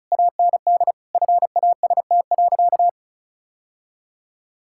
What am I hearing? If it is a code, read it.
Morse code, heard as a keyed tone, about 36 wpm, 700 Hz. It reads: AND FAST.
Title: Sentences from Top 1600 Words 35wpm